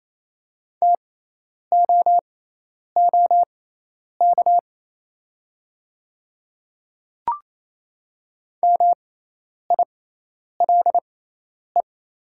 Morse code, heard as a keyed tone, about 28 wpm, spaced as for 10 wpm, 700 Hz. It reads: TOOK MILE
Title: 6x Character Spacing - Top 500 Words - 28wpm